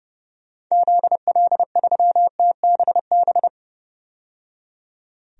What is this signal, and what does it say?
Morse code, heard as a keyed tone, about 31 wpm, 700 Hz. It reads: ZL3TBB
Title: DXCC-100 Call Signs - 6 Characters - 30wpm